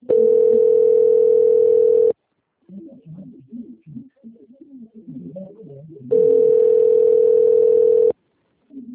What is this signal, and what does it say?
Ringback tone (North America), 440 + 480 Hz.
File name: out- - - - - .24303